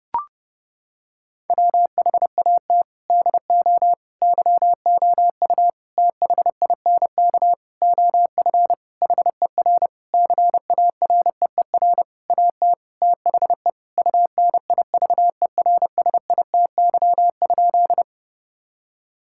Morse code, heard as a keyed tone, about 30 wpm, 700 Hz. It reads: WHAT DO YOU THINK OF HER CAREER AT THE UNIVERSITY?
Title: Sentences from Top 1000 Words - Review Part 6 - 30wpm